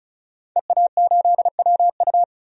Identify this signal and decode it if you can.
Morse code, heard as a keyed tone, about 36 wpm, 700 Hz. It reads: EA8WU
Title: DXCC-100 Call Signs - 5 Characters - 35wpm